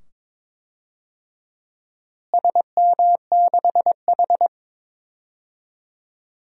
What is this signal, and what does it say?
Morse code, heard as a keyed tone, about 22 wpm, 700 Hz. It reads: SM6H
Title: DXCC-100 Call Signs - 4 Characters - Repeated 2x - 22wpm